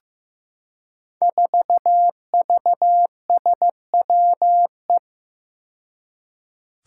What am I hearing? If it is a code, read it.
Morse code, heard as a keyed tone, about 16 wpm, 700 Hz. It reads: 4VSWE
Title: Five Challenging Letter-Number Combinations 15wpm